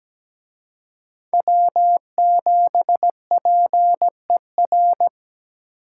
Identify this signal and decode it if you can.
Morse code, heard as a keyed tone, about 20 wpm, 700 Hz. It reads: W7PER
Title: Speed Racing - USA Call Signs 17wpm